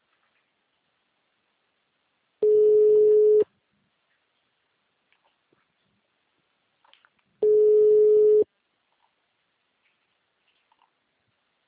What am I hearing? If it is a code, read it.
Ringback tone (Europe, 425 Hz).
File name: out- -2006- - - .23